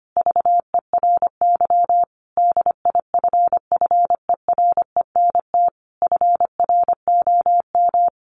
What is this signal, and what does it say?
Morse code, heard as a keyed tone, about 25 wpm, 700 Hz. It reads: VERY DIFFERENT FROM
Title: Sentences from Top 1500 Words 25wpm